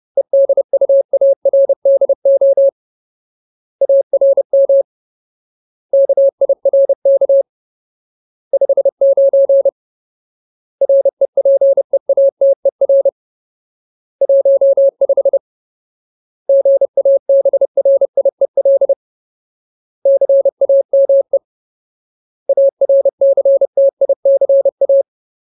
Morse code, heard as a keyed tone, about 30 wpm, 550 Hz. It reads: EDUARDO ARM KIRK 59 REPEATER 15 GABRIEL CAME ARCTICA
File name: Ultimate Word List / 30WPM Ultimate